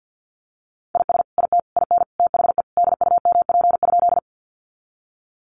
Morse code, heard as a keyed tone, about 50 wpm, 700 Hz. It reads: IS IT IN THE DUMP?
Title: Sentences from Top 1300 Words 50wpm